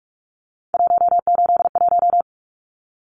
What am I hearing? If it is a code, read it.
Morse code, heard as a keyed tone, about 44 wpm, 700 Hz. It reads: 181
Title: Numbers - 3-Digit 45wpm